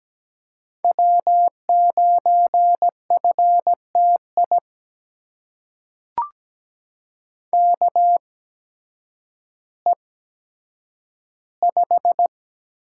Morse code, heard as a keyed tone, about 17 wpm, spaced as for 9 wpm, 700 Hz. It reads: W9FTI KE5
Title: USA Call Signs - 8x Character Spacing - 17wpm